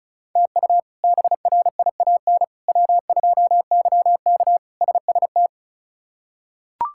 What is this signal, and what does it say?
Morse code, heard as a keyed tone, about 36 wpm, 700 Hz. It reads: TU BRIAN W2YK SST
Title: K1USN SST QSO Practice - 35wpm